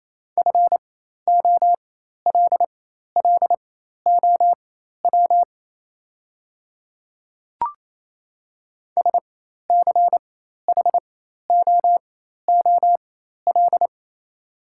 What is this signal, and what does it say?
Morse code, heard as a keyed tone, about 28 wpm, spaced as for 13 wpm, 700 Hz. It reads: FOLLOW SCHOOL
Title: 4x Character Spacing - Top 500 Words - 28wpm